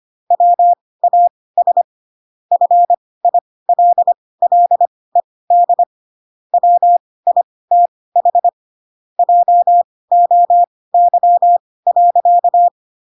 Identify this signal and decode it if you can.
Morse code, heard as a keyed tone, about 25 wpm, spaced as for 18 wpm, 700 Hz. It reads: WAS FILLED WITH JOY.
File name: Pinocchio / Pinocchio ch1 12 25 ik1hge